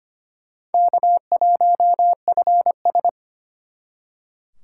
Morse code, heard as a keyed tone, about 32 wpm, 700 Hz. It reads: K1FS